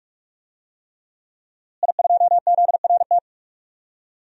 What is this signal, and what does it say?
Morse code, heard as a keyed tone, about 46 wpm, 700 Hz. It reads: I2ZRT
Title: DXCC-100 Call Signs - 5 Characters - Repeated 3x - 45wpm